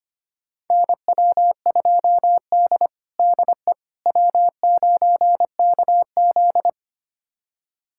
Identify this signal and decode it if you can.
Morse code, heard as a keyed tone, about 25 wpm, 700 Hz. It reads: NW2D DE W9KZ
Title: USA Callsign DE Callsign - 4 Characters - 25wpm